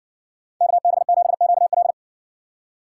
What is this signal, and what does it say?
Morse code, heard as a keyed tone, about 55 wpm, 700 Hz. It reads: KB7QL